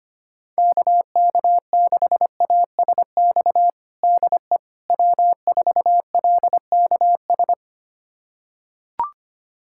Morse code, heard as a keyed tone, about 25 wpm, 700 Hz. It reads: KK6ASX DE W4LKS
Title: USA Callsign DE Callsign 25wpm